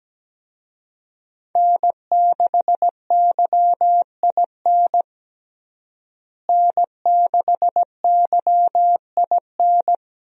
Morse code, heard as a keyed tone, about 17 wpm, 700 Hz. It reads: N6YIN N6YIN
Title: USA Call Signs - 5 Characters - Repeated 2x - 17wpm